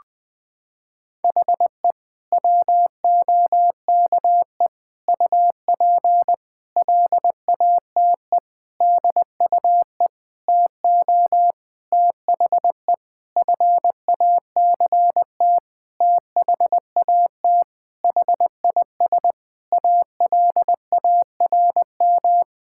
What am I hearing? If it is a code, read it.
Morse code, heard as a keyed tone, about 20 wpm, 700 Hz. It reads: HE WOKE UP LATE DUE TO THE FACT THAT HIS ALARM
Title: Sentences with 4-Word Phrases - 20wpm